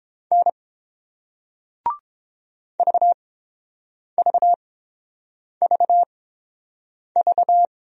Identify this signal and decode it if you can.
Morse code, heard as a keyed tone, about 31 wpm, spaced as for 7 wpm, 700 Hz. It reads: N VVVV